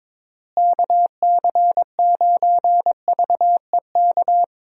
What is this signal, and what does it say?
Morse code, heard as a keyed tone, about 21 wpm, 700 Hz. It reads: KC9VEK
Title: USA Call Signs - 6 Characters - Repeated 3x - 22wpm